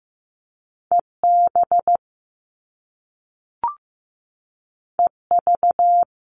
Morse code, heard as a keyed tone, about 15 wpm, 700 Hz. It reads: EB EV
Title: Top Two Letter Combinations in Words 15wpm